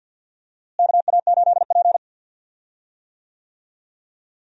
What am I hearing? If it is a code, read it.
Morse code, heard as a keyed tone, about 53 wpm, 700 Hz. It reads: KA8P
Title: USA Call Signs - 4 Characters - Repeated 2x - 50wpm